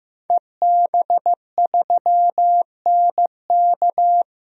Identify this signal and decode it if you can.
Morse code, heard as a keyed tone, about 15 wpm, 700 Hz. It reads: EB3NK
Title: DXCC-100 Call Signs - 5 Characters - 15wpm